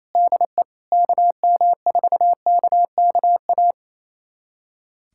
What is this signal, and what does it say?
Morse code, heard as a keyed tone, about 28 wpm, 700 Hz. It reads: DE KM4KKA
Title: USA Callsign DE Callsign - 6 Characters - 28wpm